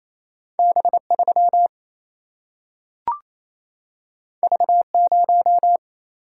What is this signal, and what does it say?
Morse code, heard as a keyed tone, about 28 wpm, 700 Hz. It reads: B3 V0
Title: Single-Letter Single-Number 28wpm